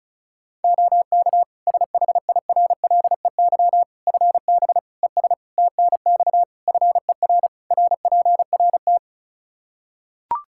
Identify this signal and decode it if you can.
Morse code, heard as a keyed tone, about 36 wpm, 700 Hz. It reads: OK SHIRLEY FB ES TNX FER RPRT
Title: Practice QSO - Exchange 2 - Info - 35wpm